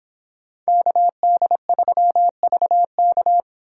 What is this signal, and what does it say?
Morse code, heard as a keyed tone, about 26 wpm, 700 Hz. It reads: KD3VK